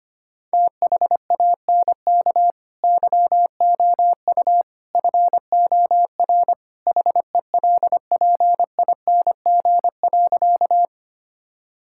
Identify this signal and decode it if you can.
Morse code, heard as a keyed tone, about 25 wpm, 700 Hz. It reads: THANK YOU FOR HELPING.
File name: practice / 25wpm / Sentences with 3-Word Phrases - 25wpm